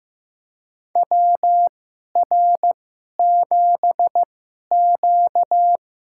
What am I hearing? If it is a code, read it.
Morse code, heard as a keyed tone, about 15 wpm, spaced as for 11 wpm, 700 Hz. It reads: WR7Q
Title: USA Call Signs - 4 Characters - 2x Character Spacing - 15wpm